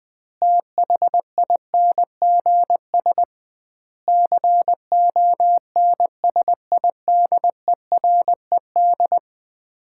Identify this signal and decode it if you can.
Morse code, heard as a keyed tone, about 20 wpm, 700 Hz. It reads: THINGS CONSIDERED
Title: 2x Word-Spacing - Sets of 5 Words - 20wpm